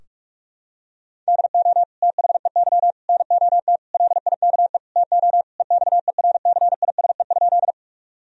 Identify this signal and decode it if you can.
Morse code, heard as a keyed tone, about 46 wpm, 700 Hz. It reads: DO THEY NOT LIKE TO EXERCISE?
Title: Sentences from Top 1000 Words - Review Part 4 - 45wpm